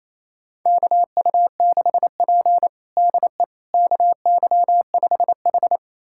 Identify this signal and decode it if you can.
Morse code, heard as a keyed tone, about 28 wpm, 700 Hz. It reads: KU6P DE KY5H
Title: USA Callsign DE Callsign - 4 Characters - 28wpm